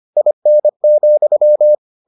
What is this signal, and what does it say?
Morse code, heard as a keyed tone, about 25 wpm, 600 Hz. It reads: IN,